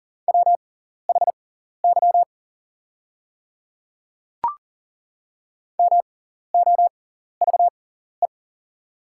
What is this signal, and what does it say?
Morse code, heard as a keyed tone, about 40 wpm, spaced as for 14 wpm, 690 Hz. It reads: WHY MOVE
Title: 6x Character Spacing - Top 200 Words - 40wpm